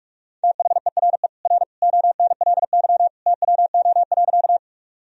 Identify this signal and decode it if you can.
Morse code, heard as a keyed tone, about 46 wpm, 700 Hz. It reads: THERE R ONLY TWO.
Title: Sentences from Top 200 Words and Abbreviations 45wpm